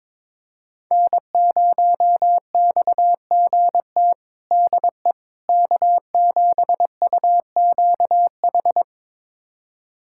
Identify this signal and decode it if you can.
Morse code, heard as a keyed tone, about 22 wpm, 700 Hz. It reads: N0XGT DE K7UQH
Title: USA Callsign DE Callsign - 5 Characters - 22wpm